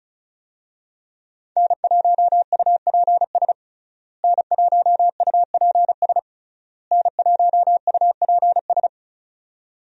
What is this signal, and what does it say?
Morse code, heard as a keyed tone, about 34 wpm, 700 Hz. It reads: N1UPS N1UPS N1UPS
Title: USA Call Signs - 5 Characters - Repeated 3x - 35wpm